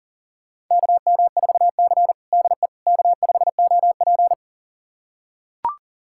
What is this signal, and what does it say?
Morse code, heard as a keyed tone, about 37 wpm, 700 Hz. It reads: KM4C DE K5OP
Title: USA Callsign DE Callsign - 4 Characters - 40wpm